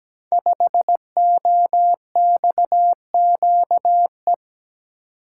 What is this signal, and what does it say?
Morse code, heard as a keyed tone, about 17 wpm, 700 Hz. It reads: 5OXQE
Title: Code Groups - 5 Letter:Numbers - 17wpm